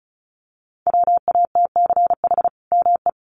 Morse code, heard as a keyed tone, about 34 wpm, 700 Hz. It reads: WATCH ME
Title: Sentences from Top 1300 Words 35wpm